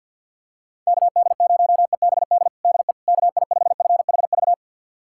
Morse code, heard as a keyed tone, about 47 wpm, 700 Hz. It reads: KD0EBD DE KI5FHV
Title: USA Callsign DE Callsign - 6 Characters - 50wpm